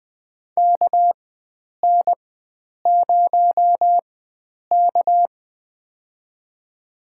Morse code, heard as a keyed tone, about 20 wpm, spaced as for 9 wpm, 700 Hz. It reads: KN0K
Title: USA Call Signs - 4x Character Spacing - 20wpm